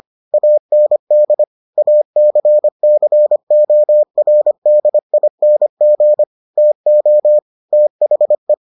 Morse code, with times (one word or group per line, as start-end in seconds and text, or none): 0.3-1.4 AND
1.8-6.2 ACCORDING
6.6-7.4 TO
7.7-8.5 THE